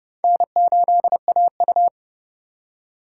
Morse code, heard as a keyed tone, about 28 wpm, 700 Hz. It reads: N8AU